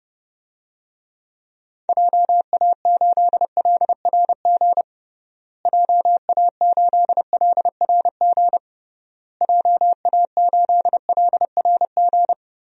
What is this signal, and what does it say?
Morse code, heard as a keyed tone, about 30 wpm, 700 Hz. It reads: JA8LRG JA8LRG JA8LRG
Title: DXCC-100 Call Signs - 6 Characters - Repeated 3x - 30wpm